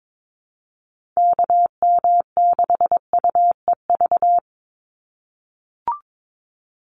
Morse code, heard as a keyed tone, about 22 wpm, 700 Hz. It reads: KM6UEV
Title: USA Call Signs 22wpm